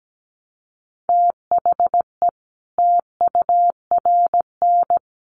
Morse code, heard as a keyed tone, about 17 wpm, 700 Hz. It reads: THE TURN
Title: Sets of 5 Words 17wpm